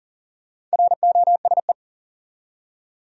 Morse code, heard as a keyed tone, about 40 wpm, 700 Hz. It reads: ROSE
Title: USA Common Names - Max 5 Letters - 40wpm